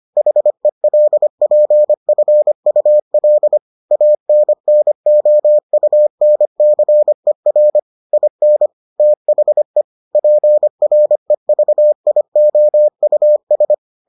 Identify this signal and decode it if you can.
Morse code, heard as a keyed tone, about 25 wpm, 600 Hz. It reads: HELPFUL ANNOUNCER IN THE PREVIOUS